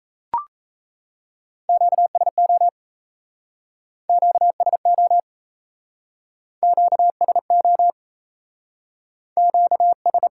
Morse code, heard as a keyed tone, about 37 wpm, 700 Hz. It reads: QSO QSO QSO QS